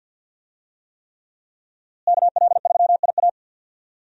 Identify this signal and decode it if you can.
Morse code, heard as a keyed tone, about 51 wpm, 700 Hz. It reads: KL3IA
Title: USA Call Signs - 5 Characters - Repeated 3x - 50wpm